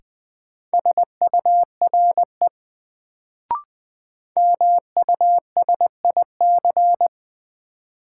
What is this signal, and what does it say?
Morse code, heard as a keyed tone, about 20 wpm, 700 Hz. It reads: SURE MUSIC
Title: Sentences from Top 300 Words - New Word Drill - 20wpm